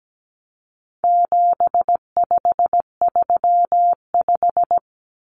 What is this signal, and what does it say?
Morse code, heard as a keyed tone, about 17 wpm, 700 Hz. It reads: 7535
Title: Common Cut Numbers - 4-Digit 17wpm